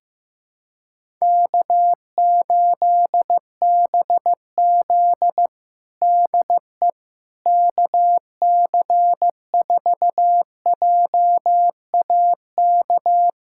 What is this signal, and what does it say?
Morse code, heard as a keyed tone, about 16 wpm, 700 Hz. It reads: K8BZ DE KC4JAK
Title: Practice QSO - Exchange 4 - Ending - 15wpm